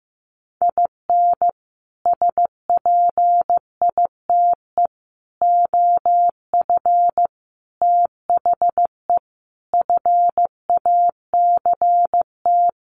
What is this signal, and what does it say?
Morse code, heard as a keyed tone, about 15 wpm, 700 Hz. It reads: IN SPITE OF THE FACT
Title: Sets of 5 Words Encore2 15wpm